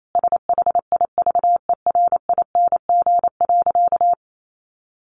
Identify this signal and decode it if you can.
Morse code, heard as a keyed tone, about 28 wpm, 700 Hz. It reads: SHIVERING.
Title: A Sound Of Thunder 28wpm